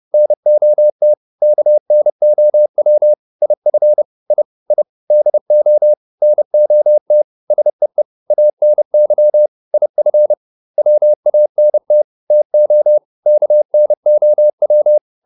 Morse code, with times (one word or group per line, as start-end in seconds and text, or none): 0.1-1.1 NOT
1.4-3.1 KNOW
3.4-4.0 IF
4.3-4.4 I
4.7-4.8 I
5.1-5.9 DO
6.2-7.2 NOT
7.5-8.0 SEE
8.3-9.5 ANY
9.7-10.3 IF
10.8-12.0 WANT
12.3-13.0 TO
13.3-15.0 KNOW